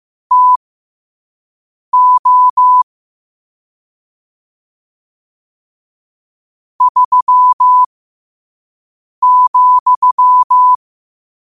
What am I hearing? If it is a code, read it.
Morse code, heard as a keyed tone, about 15 wpm, spaced as for 5 wpm, 1000 Hz. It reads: TO 3,
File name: qotdfiles / 5wpm